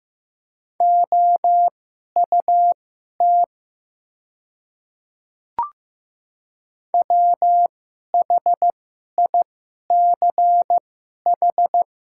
Morse code, heard as a keyed tone, about 15 wpm, spaced as for 11 wpm, 700 Hz. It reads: OUT WHICH
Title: 2x Character Spacing - Top 500 Words - 15wpm